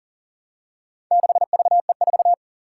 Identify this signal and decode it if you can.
Morse code, heard as a keyed tone, about 40 wpm, 700 Hz. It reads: 6VE4
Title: Four Challenging Letter-Number Combinations 40wpm